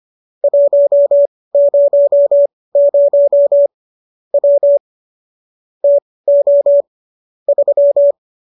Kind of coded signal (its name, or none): Morse code